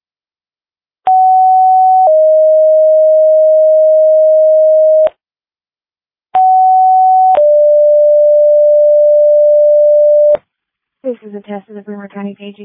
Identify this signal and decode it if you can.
Two-tone page, heard as a one-second tone, then a three-second tone, about 750 then 620 Hz, about 750 then 590 Hz, with speech. It is heard from 1.1 to 10.4 s.